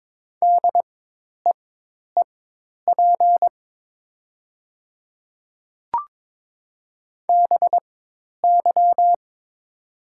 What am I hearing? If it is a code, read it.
Morse code, heard as a keyed tone, about 22 wpm, spaced as for 10 wpm, 700 Hz. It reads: DEEP BY
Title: 4x Character Spacing - Top 500 Words - 22wpm